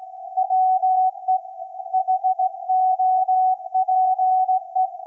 Morse code, heard as a keyed tone, about 17 wpm, 750 Hz. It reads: WE HOPE
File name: iz7slz / VLF / SAQ 24-12-2024